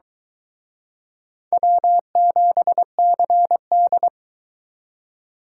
Morse code, heard as a keyed tone, about 22 wpm, 700 Hz. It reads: W7CD